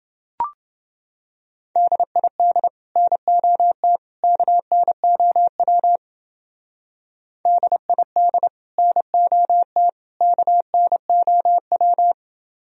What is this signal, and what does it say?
Morse code, heard as a keyed tone, about 30 wpm, 700 Hz. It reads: DID NOT KNOW DID NOT KNOW